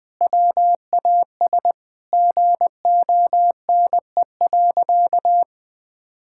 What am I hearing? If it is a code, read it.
Morse code, heard as a keyed tone, about 20 wpm, 700 Hz. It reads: WAS GONE.
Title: Sentences with 3-Word Phrases - Spoken Beginning - 20wpm